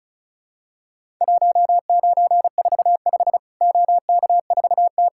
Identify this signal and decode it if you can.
Morse code, heard as a keyed tone, about 36 wpm, 700 Hz. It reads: 1945 OK4T
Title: Worked All Europe DX Contest - QTC Exchange - 35wpm